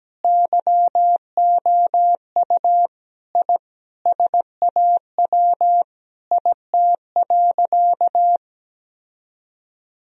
Morse code, heard as a keyed tone, about 17 wpm, 700 Hz. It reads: YOU I SAW IT.
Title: Sentences from Top 1000 Words - Review Part 4 - 17wpm